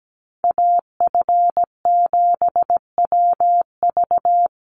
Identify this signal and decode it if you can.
Morse code, heard as a keyed tone, about 17 wpm, 700 Hz. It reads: AF7WV
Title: USA Call Signs 17wpm